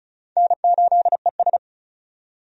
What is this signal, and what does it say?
Morse code, heard as a keyed tone, about 36 wpm, 700 Hz. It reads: N8ES
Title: USA Call Signs - 4 Characters - 35wpm